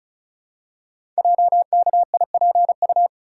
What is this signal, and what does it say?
Morse code, heard as a keyed tone, about 36 wpm, 700 Hz. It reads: JKIPU
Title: Code Groups - 5 Letters - 35wpm